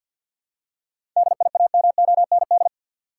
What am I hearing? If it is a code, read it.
Morse code, heard as a keyed tone, about 50 wpm, 700 Hz. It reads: DIAMOND
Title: Seven Letter Words - 50wpm